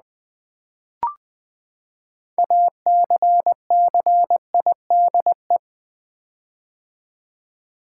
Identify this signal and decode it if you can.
Morse code, heard as a keyed tone, about 20 wpm, 700 Hz. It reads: ACCIDE
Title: Build A Word - 6 Letters At A Time - 12-Letter Words - 20wpm